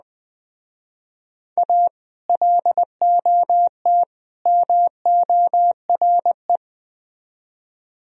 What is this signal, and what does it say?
Morse code, heard as a keyed tone, about 21 wpm, 700 Hz. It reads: A LOT MORE
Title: Speed-Racing - Sets of 3 Words 17wpm